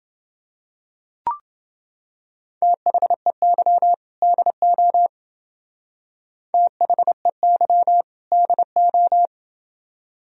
Morse code, heard as a keyed tone, about 30 wpm, 700 Hz. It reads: THEY DO THEY DO